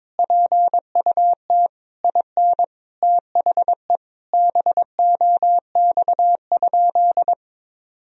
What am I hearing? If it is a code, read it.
Morse code, heard as a keyed tone, about 22 wpm, 700 Hz. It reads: PUT IN THE BOX?
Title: Sentences from Top 1000 Words - Review Part 2 - 22wpm